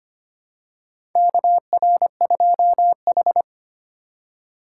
Morse code, heard as a keyed tone, about 25 wpm, 700 Hz. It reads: KR2H
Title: USA Call Signs - 4 Characters - 2x Character Spacing - 25wpm